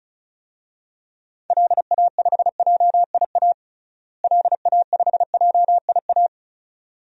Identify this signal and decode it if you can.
Morse code, heard as a keyed tone, about 34 wpm, 700 Hz. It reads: LA5JIA LA5JIA
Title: DXCC-100 Call Signs - 6 Characters - Repeated 2x - 35wpm